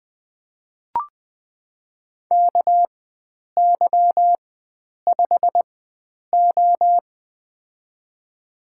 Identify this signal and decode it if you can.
Morse code, heard as a keyed tone, about 20 wpm, spaced as for 9 wpm, 700 Hz. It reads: KY5O
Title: USA Call Signs - 4 Characters - 4x Character Spacing - 20wpm